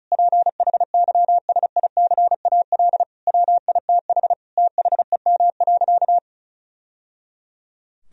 Morse code, heard as a keyed tone, about 36 wpm, 700 Hz. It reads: PHYSICAL WITH THEM.
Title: Sentences from Top 1000 Words - Review Part 4 - 35wpm